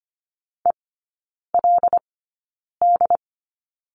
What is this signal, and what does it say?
Morse code, heard as a keyed tone, about 25 wpm, spaced as for 9 wpm, 700 Hz. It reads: ELD
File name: course / 25:10wpm / 093 Lesson 16e - Letter c - Three Characters 25wpm